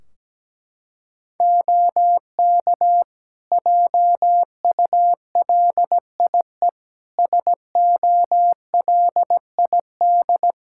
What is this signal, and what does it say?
Morse code, heard as a keyed tone, about 17 wpm, 700 Hz. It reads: OK JULIE SOLID
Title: Practice QSO - Exchange 3 - More Info - 17wpm